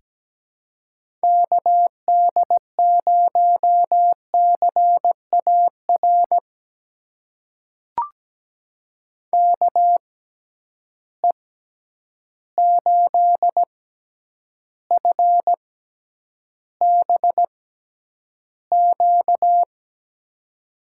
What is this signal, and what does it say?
Morse code, heard as a keyed tone, about 17 wpm, spaced as for 9 wpm, 700 Hz. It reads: KD0CAR KE8FBQ
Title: USA Call Signs - 6 Characters - 6x Character Spacing - 17wpm